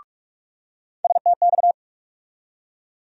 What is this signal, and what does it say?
Morse code, heard as a keyed tone, about 42 wpm, 700 Hz. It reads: STX
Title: USA Call Sign Postfixes - 3 Letters 45wpm